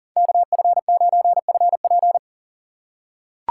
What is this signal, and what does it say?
Morse code, heard as a keyed tone, about 40 wpm, 700 Hz. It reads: KF9FP
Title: USA Call Signs - 5 Characters - 2x Character Spacing - 40wpm